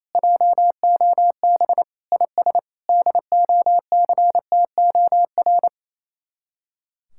Morse code, heard as a keyed tone, about 34 wpm, 700 Hz. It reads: JOB IS DOCTOR
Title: Speed-Racing Common QSO Phrases - 28wpm